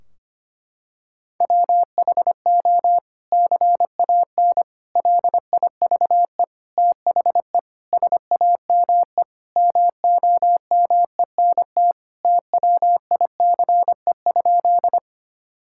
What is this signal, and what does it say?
Morse code, heard as a keyed tone, about 25 wpm, 700 Hz. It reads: WHO CAN LIVE THE SAME MOMENT TWICE?
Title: Sentences from Top 1600 Words 25wpm